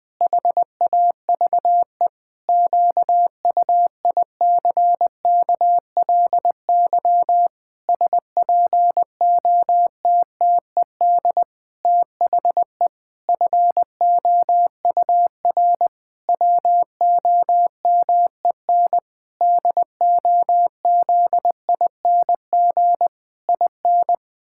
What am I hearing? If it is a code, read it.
Morse code, heard as a keyed tone, about 20 wpm, 700 Hz. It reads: HAVE QUICKLY SPOTTED THE FOUR WOMEN DOZING IN
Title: Pangram Challenge v2 - 20wpm